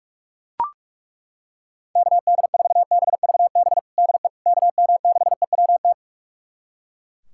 Morse code, heard as a keyed tone, about 41 wpm, 700 Hz. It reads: KD4BVB DE KM6EWT
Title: USA Callsign DE Callsign - 6 Characters - 45wpm